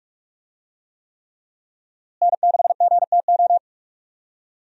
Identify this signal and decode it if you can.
Morse code, heard as a keyed tone, about 46 wpm, 700 Hz. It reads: N6GTO
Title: USA Call Signs - 5 Characters - Repeated 3x - 45wpm